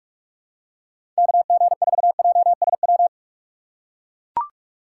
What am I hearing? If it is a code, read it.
Morse code, heard as a keyed tone, about 42 wpm, 700 Hz. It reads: KG4JSW
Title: USA Call Signs - 6 Characters - 4x Character Spacing - 45wpm